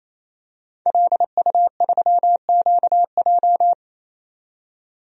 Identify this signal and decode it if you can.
Morse code, heard as a keyed tone, about 28 wpm, 700 Hz. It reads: LU3QJ
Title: DXCC-100 Call Signs - 5 Characters - 28wpm